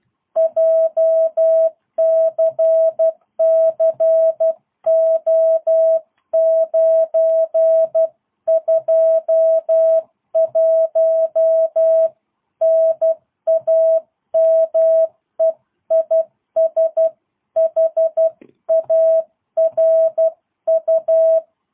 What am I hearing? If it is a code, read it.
Morse code, heard as a keyed tone, about 12 wpm, 660 Hz. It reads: JCCO921 NAME IS HARU
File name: jh8cbh / 2022cw